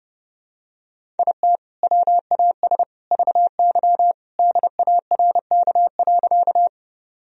Morse code, heard as a keyed tone, about 32 wpm, 700 Hz. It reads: IT WAS VY DARK.
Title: Sentences from Top 500 Words and Abbreviations 30wpm